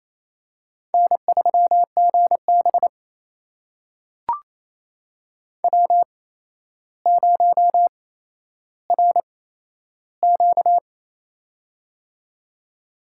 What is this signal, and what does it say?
Morse code, heard as a keyed tone, about 28 wpm, spaced as for 12 wpm, 700 Hz. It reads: N3GB W0RQ
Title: USA Call Signs - 4 Characters - 8x Character Spacing - 28wpm